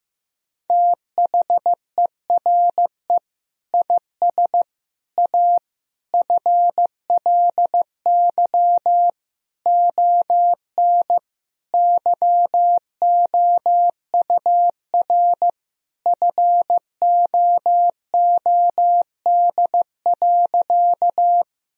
Morse code, heard as a keyed tone, about 15 wpm, 700 Hz. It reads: THERE IS A FLY ON YOUR FOOD.
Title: Sentences from Top 1000 Words - Review Part 2 - 15wpm